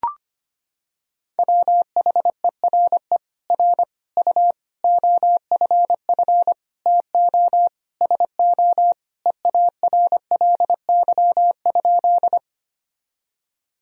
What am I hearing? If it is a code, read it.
Morse code, heard as a keyed tone, about 25 wpm, 700 Hz. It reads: WHERE R U OFF TO SO EARLY?
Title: Sentences from Top 400 Words and Abbreviations 25wpm